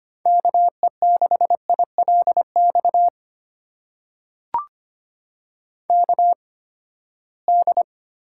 Morse code, heard as a keyed tone, about 25 wpm, spaced as for 17 wpm, 700 Hz. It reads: KE6ILX KD
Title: USA Call Signs - 6 Characters - 8x Character Spacing - 25wpm